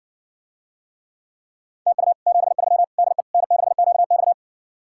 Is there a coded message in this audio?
Yes: Morse code from 1.9 to 4.3 s, heard as a keyed tone, about 60 wpm, 700 Hz. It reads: TU 73 DE N6QX